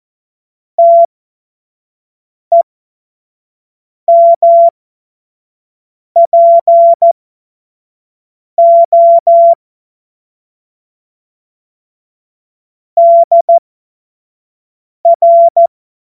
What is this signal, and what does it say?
Morse code, heard as a keyed tone, about 14 wpm, spaced as for 5 wpm, 680 Hz. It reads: TEMPO DR
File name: Morsekurs / Witze / Witz 10.Oktober 2023 25 F70